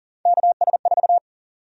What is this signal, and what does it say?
Morse code, heard as a keyed tone, about 38 wpm, 700 Hz. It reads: KS4
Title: USA Call Sign Prefixes - 40wpm